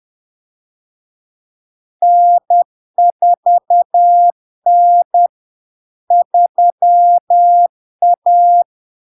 Morse code, heard as a keyed tone, about 10 wpm, 700 Hz. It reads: N4N 3A
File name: LARK-FD-2022-Log-in-CW-10wpm-700-Hz